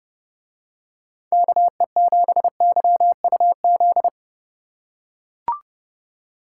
Morse code, heard as a keyed tone, about 28 wpm, 700 Hz. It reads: KE7YUZ